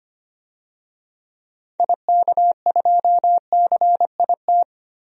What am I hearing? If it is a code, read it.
Morse code, heard as a keyed tone, about 25 wpm, 700 Hz. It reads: IK2CIT
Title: DXCC-100 Call Signs - 6 Characters - 25wpm